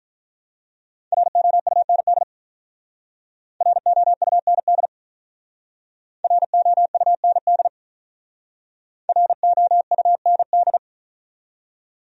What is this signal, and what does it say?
Morse code, heard as a keyed tone, about 45 wpm, 700 Hz. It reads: ROUND ROUND ROUND ROUND